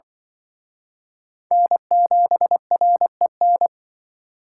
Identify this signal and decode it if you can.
Morse code, heard as a keyed tone, about 23 wpm, 700 Hz. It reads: N7REN
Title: Speed Racing - USA 1x3 Call Signs 20wpm